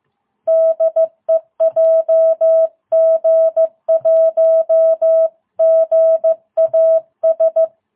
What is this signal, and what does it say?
Morse code, heard as a keyed tone, about 15 wpm, 660 Hz. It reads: DE JG1GAS